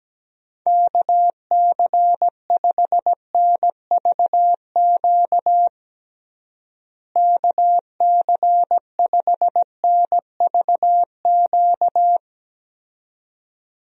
Morse code, heard as a keyed tone, about 17 wpm, 700 Hz. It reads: KC5NVQ KC5NVQ
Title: USA Call Signs - 6 Characters - Repeated 2x - 17wpm